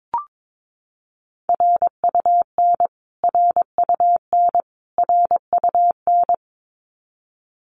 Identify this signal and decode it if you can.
Morse code, heard as a keyed tone, about 22 wpm, 700 Hz. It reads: RUN RUN RUN
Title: Top 500 Words Repeated 22wpm